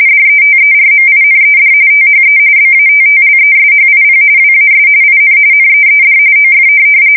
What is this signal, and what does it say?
RTTY, about 45.45 baud. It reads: DETERMINED THE BATTLESHIP WAS IN NEED OF
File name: fsk170 / USS-Massachusetts